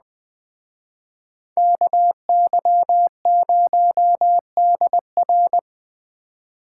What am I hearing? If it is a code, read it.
Morse code, heard as a keyed tone, about 20 wpm, 700 Hz. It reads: KY0DR